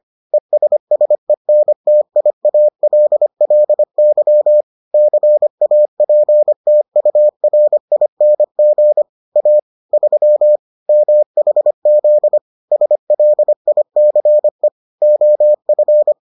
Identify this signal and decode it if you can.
Morse code, heard as a keyed tone, about 25 wpm, 600 Hz. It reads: ESSENTIALLY CAPTURING A 3 MHZ SLICE OF